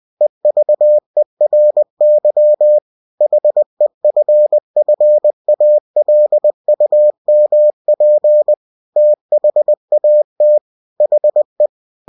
Morse code, heard as a keyed tone, about 20 wpm, 600 Hz. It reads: EVERY HEFFALUMP THAT HE
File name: cw-books / winnie-the-pooh / winnie-20wpm-page-0039